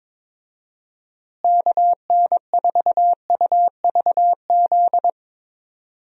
Morse code, heard as a keyed tone, about 22 wpm, 700 Hz. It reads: KN4UVZ
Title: USA Call Signs - 6 Characters - Repeated 2x - 22wpm